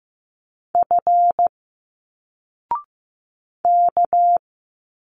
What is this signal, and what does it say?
Morse code, heard as a keyed tone, about 15 wpm, 700 Hz. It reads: F K
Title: Single Letters 15wpm